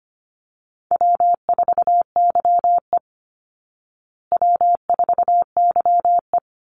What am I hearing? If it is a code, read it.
Morse code, heard as a keyed tone, about 25 wpm, 700 Hz. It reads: W4YE W4YE
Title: Call Signs Repeated 3x - 25wpm